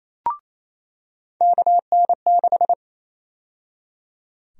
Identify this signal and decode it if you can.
Morse code, heard as a keyed tone, about 28 wpm, 700 Hz. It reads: KN6
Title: USA Call Sign Prefixes - 28wpm